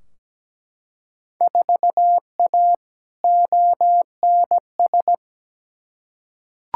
Morse code, heard as a keyed tone, about 17 wpm, 700 Hz. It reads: 4A ONS
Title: Field Day Exchanges - 17wpm